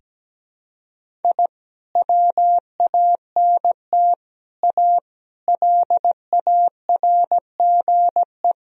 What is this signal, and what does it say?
Morse code, heard as a keyed tone, about 17 wpm, 700 Hz. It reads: I WANT A LARGE
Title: Sentences from Top 1000 Words - Review Part 1 - 17wpm